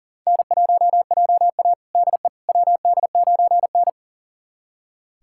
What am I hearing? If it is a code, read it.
Morse code, heard as a keyed tone, about 40 wpm, 700 Hz. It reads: N1JA DE WD9N
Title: USA Callsign DE Callsign - 4 Characters - 40wpm